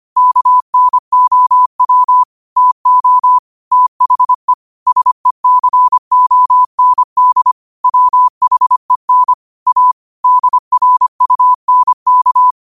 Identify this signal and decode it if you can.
Morse code, heard as a keyed tone, about 25 wpm, 1000 Hz. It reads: KNOW TO THE SECOND WHEN A DRUNK